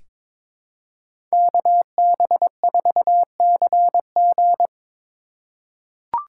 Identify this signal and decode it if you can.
Morse code, heard as a keyed tone, about 22 wpm, 700 Hz. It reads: KB4CG
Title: K1USN SST QSO Practice - 22wpm